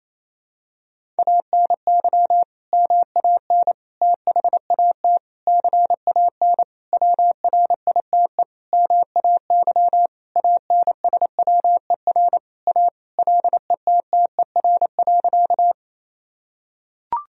Morse code, heard as a keyed tone, about 28 wpm, 700 Hz. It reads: ANY MAN THAT CAN WRITE MAY ANSWER A LETTER.
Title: Sentences from Top 1000 Words - Review Part 1 - 28wpm